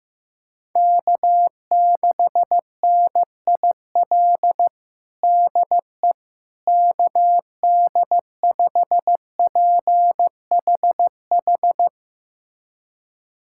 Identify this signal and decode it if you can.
Morse code, heard as a keyed tone, about 15 wpm, 700 Hz. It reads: K6NIL DE KD5PHH
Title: USA Callsign DE Callsign 15wpm